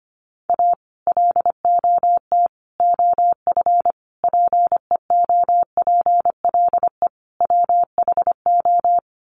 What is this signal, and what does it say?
Morse code, heard as a keyed tone, about 25 wpm, 700 Hz. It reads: A LOT OF PEOPLE WHO
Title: Sets of 5 Words 25wpm